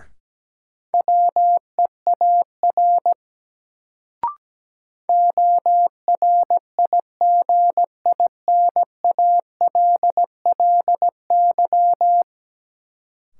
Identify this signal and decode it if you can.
Morse code, heard as a keyed tone, about 17 wpm, 700 Hz. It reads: WEAR ORIGINALLY
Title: Sentences from Top 1500 Words - New Word Drill Alt - 17wpm